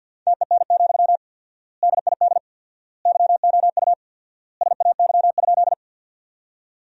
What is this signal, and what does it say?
Morse code, heard as a keyed tone, about 49 wpm, 700 Hz. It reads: TEN, DID YOU SAY?
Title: 4x Word-Spacing - Sentences from Top 400 Words - 50wpm